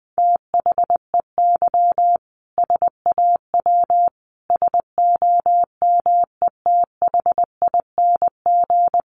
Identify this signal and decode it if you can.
Morse code, heard as a keyed tone, about 20 wpm, 700 Hz. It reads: THEY SAW SOMETHING
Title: Sentences from Top 600 Words 20wpm